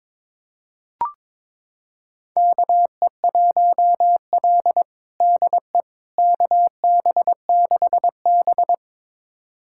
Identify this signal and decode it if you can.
Morse code, heard as a keyed tone, about 22 wpm, 700 Hz. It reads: KE1L DE KB6B
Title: USA Callsign DE Callsign - 4 Characters - 22wpm